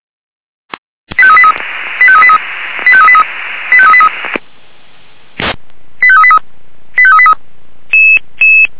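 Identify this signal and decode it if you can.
Five-tone selective call (ZVEI), heard as quick runs of five tones, tones of about 70 ms. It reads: 85383 85383 85383 85383 85383 85383
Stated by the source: Five-tone ZVEI selective call 85383, received by SDR.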